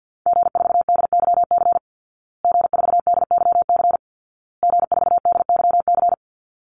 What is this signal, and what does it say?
Morse code, heard as a keyed tone, about 50 wpm, 700 Hz. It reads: G4DYC G4DYC G4DYC
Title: Call Signs Repeated 2x - 50wpm